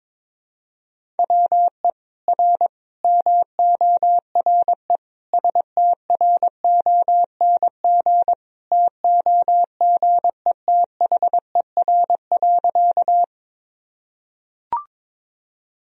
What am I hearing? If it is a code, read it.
Morse code, heard as a keyed tone, about 22 wpm, 700 Hz. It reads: WE R MORE STRONG TOGETHER.
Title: Sentences from Top 500 Words and Abbreviations 22wpm